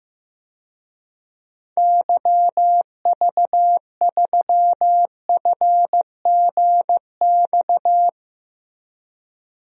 Morse code, heard as a keyed tone, about 15 wpm, 700 Hz. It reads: YV3FGX
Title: DXCC-100 Call Signs - 6 Characters - Repeated 2x - 15wpm